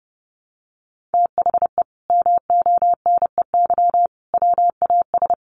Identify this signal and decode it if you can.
Morse code, heard as a keyed tone, about 30 wpm, 700 Hz. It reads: THE MONEY WAS
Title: Sentences from Top 1400 Words 30wpm